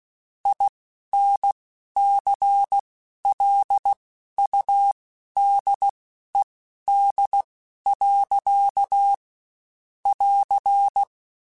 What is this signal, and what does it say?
Morse code, heard as a keyed tone, about 16 wpm, spaced as for 11 wpm, 800 Hz. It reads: INCLUDED. <AR>
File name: Quotes Spaced / Quotes Spaced-mp3s / Airtight sp @15